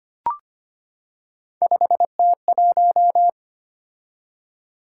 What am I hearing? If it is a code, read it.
Morse code, heard as a keyed tone, about 25 wpm, 700 Hz. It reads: 5T1
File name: Three Letter-Number Combinations - 25wpm